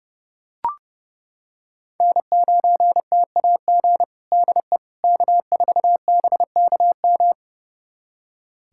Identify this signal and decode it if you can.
Morse code, heard as a keyed tone, about 30 wpm, 700 Hz. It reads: N9TAG DE K4BKM
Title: USA Callsign DE Callsign - 5 Characters - 30wpm